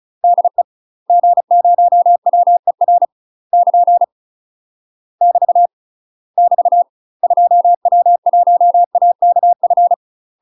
Morse code, heard as a keyed tone, about 34 wpm, 700 Hz. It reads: DE G0WER <KN> <BT> <BT> 2W1AKF
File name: UK MASTER QSOs TRAINING / 35WPM / UK MASTER QSOs 0005